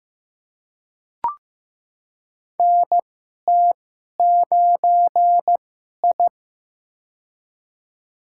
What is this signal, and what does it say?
Morse code, heard as a keyed tone, about 15 wpm, spaced as for 11 wpm, 700 Hz. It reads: NT9I
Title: USA Call Signs - 2x Character Spacing - 15wpm